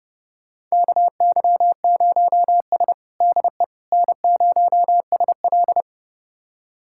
Morse code, heard as a keyed tone, about 30 wpm, 700 Hz. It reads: KY0S DE N0SL
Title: USA Callsign DE Callsign - 4 Characters - 30wpm